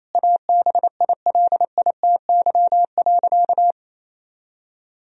Morse code, heard as a keyed tone, about 28 wpm, 700 Hz. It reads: ABILITY.
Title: Sentences from Top 1000 Words - Review Part 3 - 28wpm